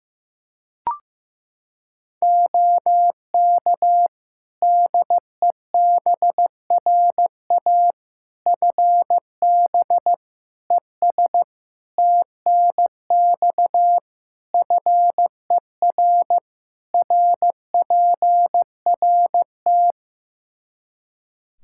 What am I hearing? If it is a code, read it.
Morse code, heard as a keyed tone, about 15 wpm, 700 Hz. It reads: OK DEBRA FB ES TNX FER RPRT
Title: Practice QSO - Exchange 2 - Info - 15wpm